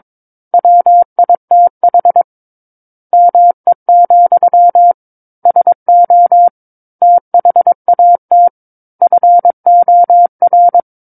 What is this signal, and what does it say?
Morse code, heard as a keyed tone, about 22 wpm, 700 Hz. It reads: WITH ME, SO THAT FOR
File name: Book One / 13-Chapter 13 - 22 WPM